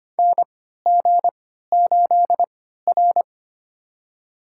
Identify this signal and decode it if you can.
Morse code, heard as a keyed tone, about 25 wpm, spaced as for 14 wpm, 700 Hz. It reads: NG8R
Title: USA Call Signs - 3x Character Spacing - 25wpm